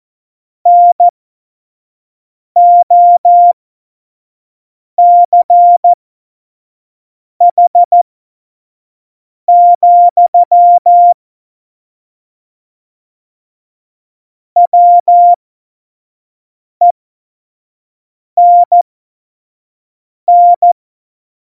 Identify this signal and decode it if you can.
Morse code, heard as a keyed tone, about 14 wpm, spaced as for 5 wpm, 700 Hz. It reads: NOCH, WENN
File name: Morsekurs / Witze / Witz 19. Januar 2021, 25 F70